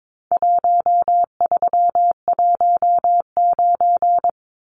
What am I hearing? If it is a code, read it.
Morse code, heard as a keyed tone, about 22 wpm, 700 Hz. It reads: 1319
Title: Numbers - Contest Style - 4-Digit 22wpm